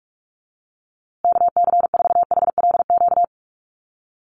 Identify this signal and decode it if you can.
Morse code, heard as a keyed tone, about 44 wpm, 700 Hz. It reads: KC4HLQ
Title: USA Call Signs Repeated 3x 45wpm